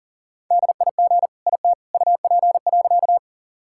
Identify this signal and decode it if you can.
Morse code, heard as a keyed tone, about 40 wpm, 700 Hz. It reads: DIG IT UP.
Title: Sentences from Top 1600 Words 40wpm